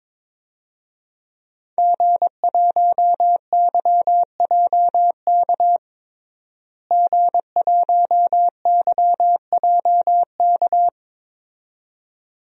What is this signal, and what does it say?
Morse code, heard as a keyed tone, about 22 wpm, 700 Hz. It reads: G1YJK G1YJK
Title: DXCC-100 Call Signs - 5 Characters - Repeated 2x - 22wpm